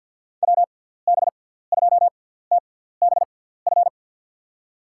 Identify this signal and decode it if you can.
Morse code, heard as a keyed tone, about 47 wpm, spaced as for 17 wpm, 700 Hz. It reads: WB2TBF
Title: USA Call Signs - 6 Characters - 6x Character Spacing - 50wpm